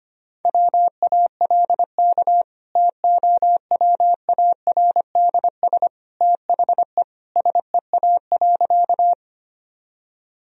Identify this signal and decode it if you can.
Morse code, heard as a keyed tone, about 25 wpm, 700 Hz. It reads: WALK TOWARDS THE SEA.
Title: Sentences from Top 1000 Words - Review Part 2 - 25wpm